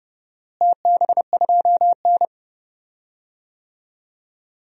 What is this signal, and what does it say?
Morse code, heard as a keyed tone, about 30 wpm, 700 Hz. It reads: TB2N
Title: DXCC-100 Call Signs - 4 Characters - Repeated 3x - 30wpm